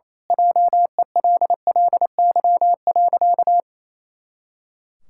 Morse code, heard as a keyed tone, about 27 wpm, 700 Hz. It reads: JELLY.
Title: Pangram Challenge v2 - 28wpm